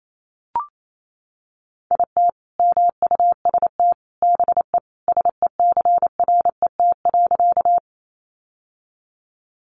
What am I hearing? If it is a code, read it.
Morse code, heard as a keyed tone, about 28 wpm, 700 Hz. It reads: IT MUST BE SECRET.
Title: Sentences from Top 1200 Words 28wpm